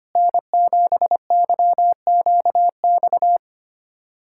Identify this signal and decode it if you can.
Morse code, heard as a keyed tone, about 25 wpm, 700 Hz. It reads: N7YQX
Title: USA Callsign DE Callsign - 5 Characters - 25wpm